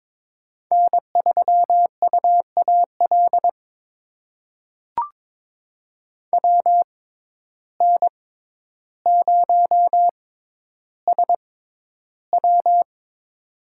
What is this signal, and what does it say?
Morse code, heard as a keyed tone, about 22 wpm, spaced as for 11 wpm, 700 Hz. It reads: N3UAL WN0SW
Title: USA Call Signs - 5 Characters - 6x Character Spacing - 22wpm